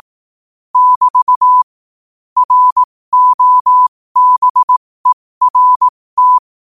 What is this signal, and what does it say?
Morse code, heard as a keyed tone, about 18 wpm, spaced as for 15 wpm, 1000 Hz. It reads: <BT> ROBERT
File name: qotdfiles / 15wpm